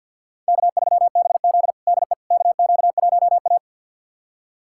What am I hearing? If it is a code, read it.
Morse code, heard as a keyed tone, about 47 wpm, 700 Hz. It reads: K3BZ DE KQ1A
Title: USA Callsign DE Callsign - 4 Characters - 50wpm